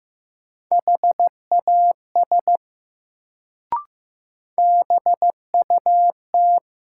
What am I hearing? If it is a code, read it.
Morse code, heard as a keyed tone, about 15 wpm, 700 Hz. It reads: HAS BUT
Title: Sentences from Top 100 Words - New Word Drill - 15wpm